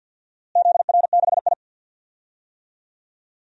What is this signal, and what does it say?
Morse code, heard as a keyed tone, about 46 wpm, 700 Hz. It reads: ZR6I
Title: DXCC-100 Call Signs - 4 Characters - Repeated 3x - 50wpm